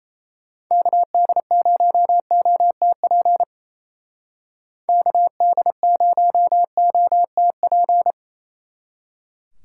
Morse code, heard as a keyed tone, about 33 wpm, 700 Hz. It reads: KD0OTP KD0OTP
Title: Speed Racing - USA 2x3 Call Signs 28wpm